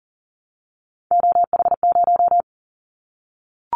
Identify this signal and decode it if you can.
Morse code, heard as a keyed tone, about 38 wpm, 700 Hz. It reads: OH0
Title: Callsign Prefixes for Top 100 DXCC - 40wpm